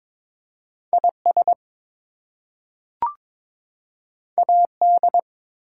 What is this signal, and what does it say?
Morse code, heard as a keyed tone, about 22 wpm, 700 Hz. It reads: IS AD
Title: Two Letter Words - 22wpm